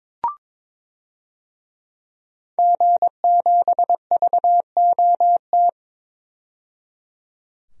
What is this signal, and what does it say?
Morse code, heard as a keyed tone, about 22 wpm, 700 Hz. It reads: G7VOT
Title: DXCC-100 Call Signs - 5 Characters - 22wpm